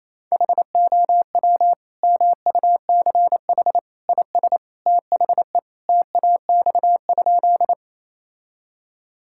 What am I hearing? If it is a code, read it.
Morse code, heard as a keyed tone, about 28 wpm, 700 Hz. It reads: HOW MUCH IS THE TAX?
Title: Sentences from Top 1000 Words - Review Part 5 - 28wpm